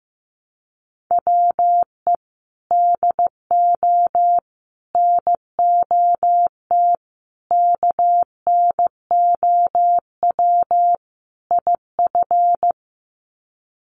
Morse code, heard as a keyed tone, about 15 wpm, 700 Hz. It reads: WE DO NOT KNOW IF
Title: Sets of 5 Words Encore2 15wpm